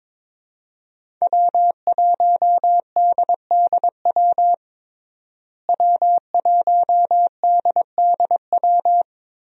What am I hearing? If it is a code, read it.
Morse code, heard as a keyed tone, about 22 wpm, 700 Hz. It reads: W1DDW W1DDW
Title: USA Call Signs - 5 Characters - Repeated 2x - 22wpm